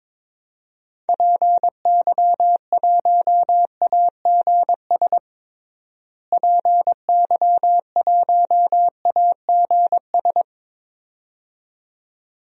Morse code, heard as a keyed tone, about 22 wpm, 700 Hz. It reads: PY1AGS PY1AGS
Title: DXCC-100 Call Signs - 6 Characters - Repeated 2x - 22wpm